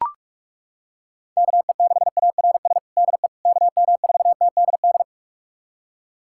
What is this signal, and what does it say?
Morse code, heard as a keyed tone, about 42 wpm, 700 Hz. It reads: KE6ARS DE KM4TDD
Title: USA Callsign DE Callsign - 6 Characters - 45wpm